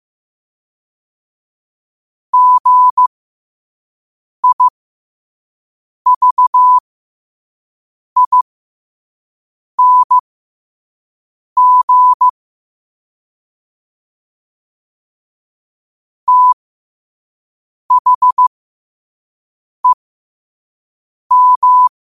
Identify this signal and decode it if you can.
Morse code, heard as a keyed tone, about 15 wpm, spaced as for 5 wpm, 1000 Hz. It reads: GIVING THEM
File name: qotdfiles / 5wpm